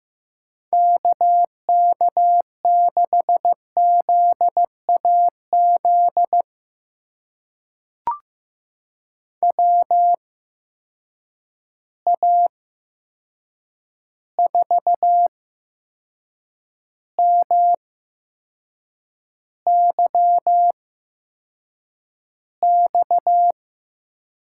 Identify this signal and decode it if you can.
Morse code, heard as a keyed tone, about 15 wpm, spaced as for 6 wpm, 700 Hz. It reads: KK6ZAZ WA4MYX
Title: USA Call Signs - 6 Characters - 8x Character Spacing - 15wpm